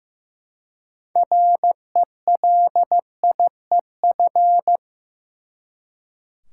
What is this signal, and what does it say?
Morse code, heard as a keyed tone, about 15 wpm, 700 Hz. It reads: RELIEF
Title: Six Letter Words - 15wpm